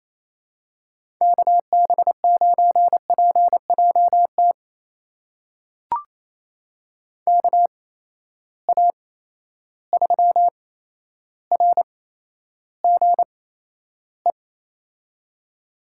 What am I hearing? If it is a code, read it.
Morse code, heard as a keyed tone, about 28 wpm, spaced as for 12 wpm, 700 Hz. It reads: KB9PJT KA3RGE
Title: USA Call Signs - 6 Characters - 8x Character Spacing - 28wpm